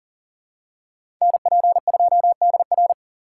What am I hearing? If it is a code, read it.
Morse code, heard as a keyed tone, about 40 wpm, 700 Hz. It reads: NP2DR
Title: USA Call Signs - 5 Characters - 2x Character Spacing - 40wpm